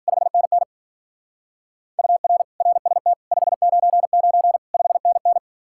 Morse code, heard as a keyed tone, about 49 wpm, 700 Hz. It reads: 5NN UR RST 599 5NN
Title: Speed-Racing Common QSO Phrases - 40wpm